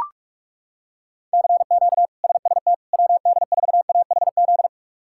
Morse code, heard as a keyed tone, about 42 wpm, 700 Hz. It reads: CQ SST WD4AHZ
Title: K1USN SST QSO Practice - 45wpm